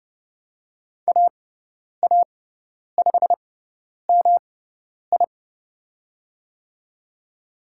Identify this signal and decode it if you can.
Morse code, heard as a keyed tone, about 29 wpm, spaced as for 10 wpm, 700 Hz. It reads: AA5MI
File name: Lesson 18g - Letter m - Call Signs 30wpm